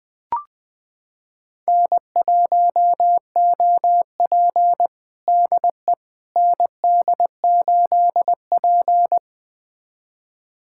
Morse code, heard as a keyed tone, about 21 wpm, 700 Hz. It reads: N1OP DE ND8P
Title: USA Callsign DE Callsign - 4 Characters - 20wpm